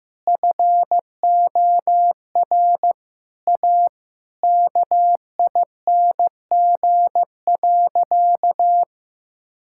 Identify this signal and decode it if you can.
Morse code, heard as a keyed tone, about 15 wpm, 700 Hz. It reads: FOR A KING.
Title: Sentences from Top 1000 Words - Review Part 5 - 15wpm